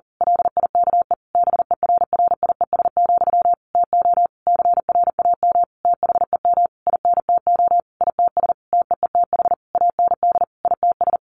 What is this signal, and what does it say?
Morse code, heard as a keyed tone, about 40 wpm, 700 Hz. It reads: LIKE BERRIES, TO CRAM THEM INTO ITS TEETH AND ITS
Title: A Sound Of Thunder 40wpm